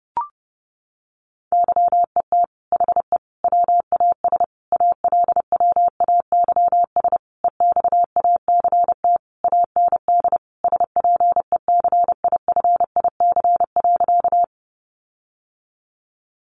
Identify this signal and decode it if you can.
Morse code, heard as a keyed tone, about 30 wpm, 700 Hz. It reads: YET HE WAS ALWAYS EXACT AND SPECIFIC.
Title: Sentences from Top 1300 Words 30wpm